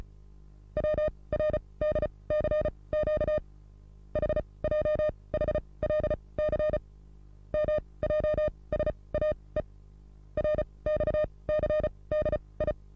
Morse code, heard as a keyed tone, about 35 wpm, spaced as for 23 wpm, 610 Hz. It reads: WRDCQ HJHLC MJSAE RXCDI
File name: repos / Morsekurs / DJ4UF Morsen27